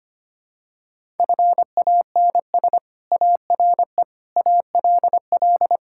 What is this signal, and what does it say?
Morse code, heard as a keyed tone, about 25 wpm, 700 Hz. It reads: FANS ARE ALL
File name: practice / 25wpm / Sentences with 4-Word Phrases - 25wpm